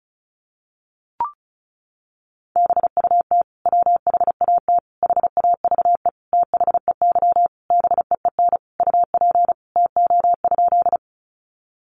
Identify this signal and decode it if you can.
Morse code, heard as a keyed tone, about 34 wpm, 700 Hz. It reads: BUT WHAT HAVE THEY BEEN UP TO?
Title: Sentences from Top 100 Words 35wpm